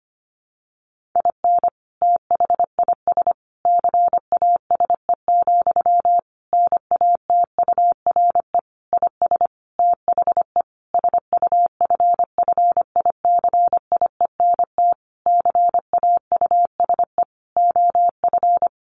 Morse code, heard as a keyed tone, about 25 wpm, 700 Hz. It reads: IN THIS CASE, NATURE IS THE SUFFICIENT CAUSE OF
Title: Sentences from Top 1400 Words 25wpm